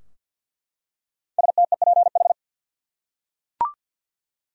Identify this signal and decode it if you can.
Morse code, heard as a keyed tone, about 48 wpm, 700 Hz. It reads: STEPH
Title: USA Common Names - 50wpm